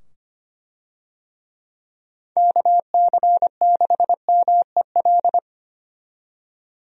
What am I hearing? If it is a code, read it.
Morse code, heard as a keyed tone, about 25 wpm, 700 Hz. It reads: KC6MEL
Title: USA Call Signs - 6 Characters - 25wpm